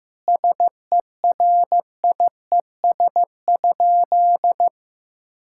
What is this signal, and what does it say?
Morse code, heard as a keyed tone, about 15 wpm, 700 Hz. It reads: SERIES?
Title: Sentences from Top 1000 Words - Review Part 5 - 15wpm